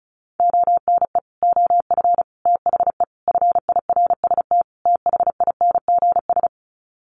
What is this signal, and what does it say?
Morse code, heard as a keyed tone, about 34 wpm, 700 Hz. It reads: ONE OF THE FIRST THINGS
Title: Sets of 5 Words Encore 35wpm